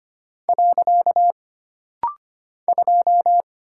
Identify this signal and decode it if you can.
Morse code, heard as a keyed tone, about 25 wpm, 700 Hz. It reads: . 2
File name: Single Character (Letter:Number:Punctuation:Prosign) - Rapid-Fire - 25wpm